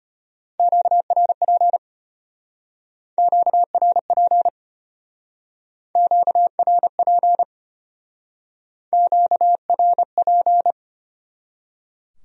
Morse code, heard as a keyed tone, about 30 wpm, 700 Hz. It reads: QRP QRP QRP QRP
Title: Speed-Racing Q-Codes - 25wpm